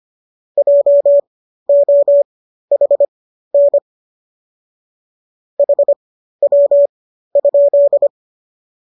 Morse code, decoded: JOHN HW?

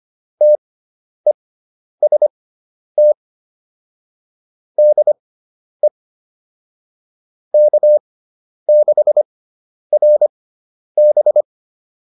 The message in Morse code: TEST DE K6RB

